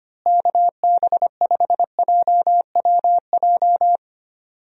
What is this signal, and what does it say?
Morse code, heard as a keyed tone, about 25 wpm, 700 Hz. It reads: KB5JWJ
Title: USA Call Signs - 6 Characters - Repeated 2x - 25wpm